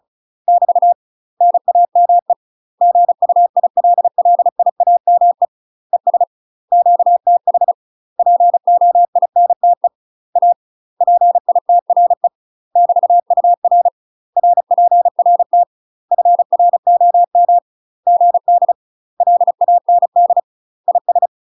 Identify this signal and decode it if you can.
Morse code, heard as a keyed tone, about 36 wpm, 700 Hz. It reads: <BT> NAME GUILLIAME ES QTH POINTE A PITRE <BT>UR RPRT FROM GD LAND IS
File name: UK MASTER QSOs TRAINING / 35WPM / UK MASTER QSOs 0020